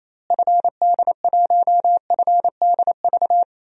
Morse code, heard as a keyed tone, about 27 wpm, 700 Hz. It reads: FD1FDV
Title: DXCC-100 Call Signs - 6 Characters - Repeated 3x - 28wpm